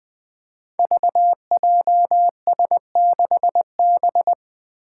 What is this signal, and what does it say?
Morse code, heard as a keyed tone, about 21 wpm, 700 Hz. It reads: VJS6B
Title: Five Challenging Letter-Number Combinations 20wpm